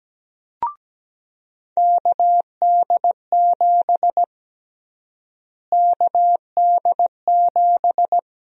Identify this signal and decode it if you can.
Morse code, heard as a keyed tone, about 17 wpm, 700 Hz. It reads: KD7 KD7
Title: USA Call Sign Prefixes - Repeated 2x - 17wpm